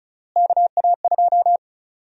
Morse code, heard as a keyed tone, about 34 wpm, 700 Hz. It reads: KA2
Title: USA Call Sign Prefixes - Repeated 2x - 35wpm